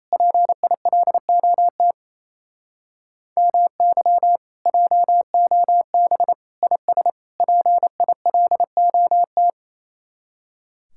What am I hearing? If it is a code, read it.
Morse code, heard as a keyed tone, about 36 wpm, 700 Hz. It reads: PILOT MY JOB IS PILOT